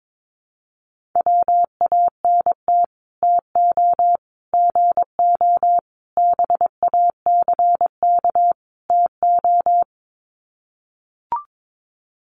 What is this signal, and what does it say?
Morse code, heard as a keyed tone, about 22 wpm, 700 Hz. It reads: WANT TO GO BACK TO
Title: Sets of 5 Words 22wpm